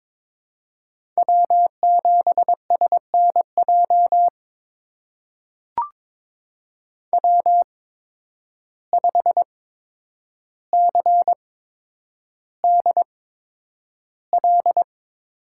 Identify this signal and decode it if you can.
Morse code, heard as a keyed tone, about 22 wpm, spaced as for 9 wpm, 700 Hz. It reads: W7SNJ W5CDL
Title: USA Call Signs - 5 Characters - 8x Character Spacing - 22wpm